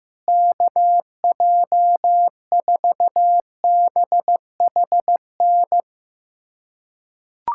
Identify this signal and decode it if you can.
Morse code, heard as a keyed tone, about 15 wpm, 700 Hz. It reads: KJ4BHN
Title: USA Callsign DE Callsign 15wpm